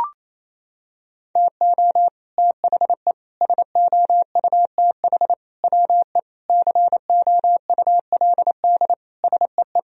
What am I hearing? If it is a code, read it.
Morse code, heard as a keyed tone, about 28 wpm, 700 Hz. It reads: TO THE SOUTH WE COULD SEE
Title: Sentences from Top 1000 Words - Review Part 2 - 28wpm